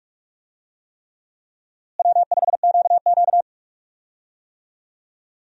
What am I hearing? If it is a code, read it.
Morse code, heard as a keyed tone, about 46 wpm, 700 Hz. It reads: W5QQ
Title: USA Call Signs - 4 Characters - Repeated 3x - 45wpm